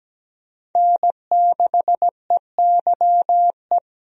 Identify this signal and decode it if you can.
Morse code, heard as a keyed tone, about 17 wpm, 700 Hz. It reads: N6EYE
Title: USA Call Signs - 5 Characters - 17wpm